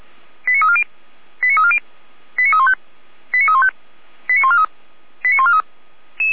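Five-tone selective call (ZVEI), heard as quick runs of five tones, tones of about 70 ms. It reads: 89340 89340 89315 89315 89143 89143
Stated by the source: ZVEI five-tone call received by SDR, logged on server of feuerwehr-orlamuende.org (Feuerwehr Orlamünde).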